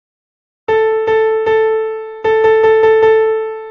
There is no code.